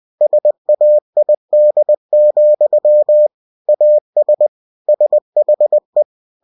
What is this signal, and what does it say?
Morse code, heard as a keyed tone, about 20 wpm, 600 Hz. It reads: SAID, AS SHE
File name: cw-books / winnie-the-pooh / winnie-20wpm-page-0071